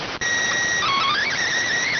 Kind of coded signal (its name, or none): SSTV (slow-scan television)